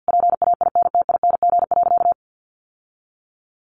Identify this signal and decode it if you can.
Morse code, heard as a keyed tone, about 50 wpm, 700 Hz. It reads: PAINTING.
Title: Sentences from Top 1200 Words 50wpm